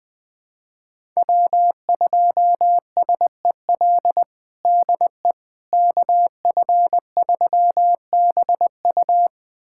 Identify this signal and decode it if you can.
Morse code, heard as a keyed tone, about 20 wpm, 700 Hz. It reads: W2SEL DE KF3BU
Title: USA Callsign DE Callsign - 5 Characters - 20wpm